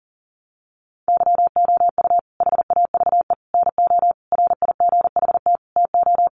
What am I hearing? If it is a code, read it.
Morse code, heard as a keyed tone, about 37 wpm, 700 Hz. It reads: YOU HAVE NO RIGHT TO
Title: Sentences from Top 1100 Words 40wpm